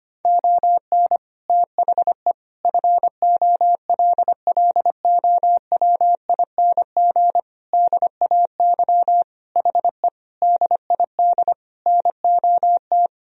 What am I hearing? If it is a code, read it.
Morse code, heard as a keyed tone, about 25 wpm, 700 Hz. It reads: ON THE FOLLOWING DAY HE DID NOT
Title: Sentences from Top 1000 Words - Review Part 4 - 25wpm